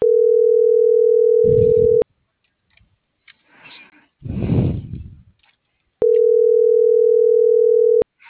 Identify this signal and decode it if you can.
Ringback tone (North America), 440 + 480 Hz.